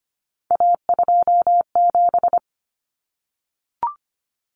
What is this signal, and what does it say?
Morse code, heard as a keyed tone, about 25 wpm, 700 Hz. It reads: A27
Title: Common Cut Numbers - 3-Digit 25wpm